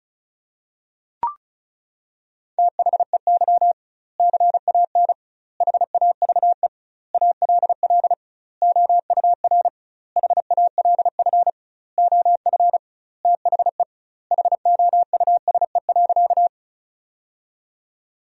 Morse code, heard as a keyed tone, about 36 wpm, 700 Hz. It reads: THEY CAN HAVE ALL OUR HALF OF THE HOUSE.
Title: 2x Word-Spacing - Sentences from Top 400 Words - 35wpm